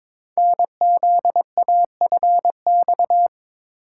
Morse code, heard as a keyed tone, about 22 wpm, 700 Hz. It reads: NZAFX
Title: Code Groups - 5 Letters - 22wpm